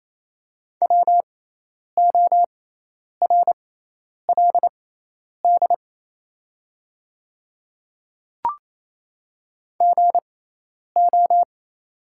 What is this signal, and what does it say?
Morse code, heard as a keyed tone, about 28 wpm, spaced as for 10 wpm, 700 Hz. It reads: WORLD GO
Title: 6x Character Spacing - Top 200 Words - 28wpm